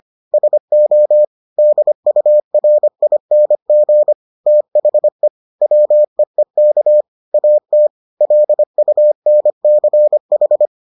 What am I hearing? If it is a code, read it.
Morse code, heard as a keyed tone, about 25 wpm, 600 Hz. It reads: SO DURING THE WEEK AT LUNCH